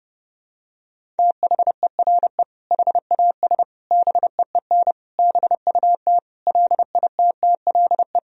Morse code, heard as a keyed tone, about 30 wpm, 700 Hz. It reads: THERE HAS BEEN BUT LITTLE
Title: Sentences from Top 1000 Words - Review Part 1 - 30wpm